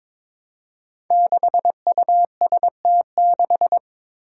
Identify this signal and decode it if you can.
Morse code, heard as a keyed tone, about 22 wpm, 700 Hz. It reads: 6UST6
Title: Five Challenging Letter-Number Combinations 22wpm